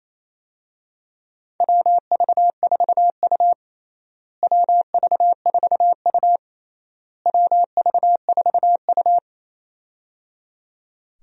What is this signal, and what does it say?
Morse code, heard as a keyed tone, about 27 wpm, 700 Hz. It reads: WV4U WV4U WV4U
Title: USA Call Signs - 4 Characters - Repeated 3x - 28wpm